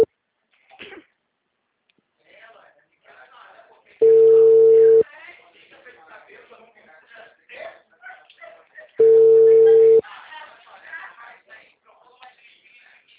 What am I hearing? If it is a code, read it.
Ringback tone (China, 450 Hz), with speech.